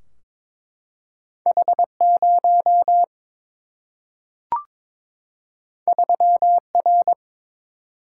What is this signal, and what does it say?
Morse code, heard as a keyed tone, about 22 wpm, 700 Hz. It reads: H0 3R
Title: Single-Letter Single-Number 22wpm